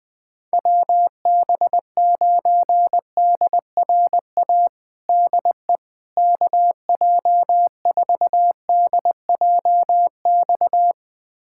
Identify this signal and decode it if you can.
Morse code, heard as a keyed tone, about 20 wpm, 700 Hz. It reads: WB9DRA DE KJ4DJX
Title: USA Callsign DE Callsign - 6 Characters - 20wpm